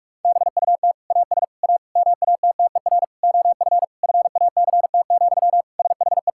Morse code, heard as a keyed tone, about 46 wpm, 700 Hz. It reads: BUT AS A MATTER OF FACT, SHE
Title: Sentences from Top 1000 Words - Review Part 6 - 45wpm